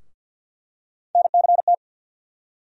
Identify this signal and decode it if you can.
Morse code, heard as a keyed tone, about 52 wpm, 690 Hz. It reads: NCT
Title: USA Call Sign Postfixes - 3 Letters 50wpm